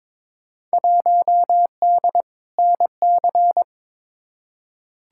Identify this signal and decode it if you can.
Morse code, heard as a keyed tone, about 22 wpm, 700 Hz. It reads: 1D NC
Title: Field Day Exchanges - 22wpm